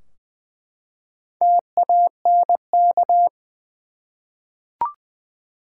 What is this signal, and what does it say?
Morse code, heard as a keyed tone, about 21 wpm, 700 Hz. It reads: TANK
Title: Four Letter Words - 20wpm